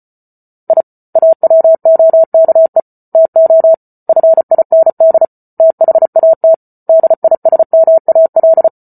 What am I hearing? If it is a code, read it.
Morse code, heard as a keyed tone, about 34 wpm, 650 Hz. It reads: I AWOKE TO FIND THAT DISMAL